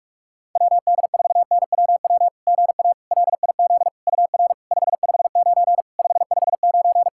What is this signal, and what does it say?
Morse code, heard as a keyed tone, about 44 wpm, 690 Hz. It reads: WD4NWW GA LIZ UR 559 559
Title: SOTA QSO Practice 45wpm